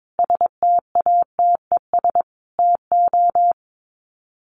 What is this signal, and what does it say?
Morse code, heard as a keyed tone, about 22 wpm, 700 Hz. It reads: STATES TO
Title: Sets of 5 Words Encore 22wpm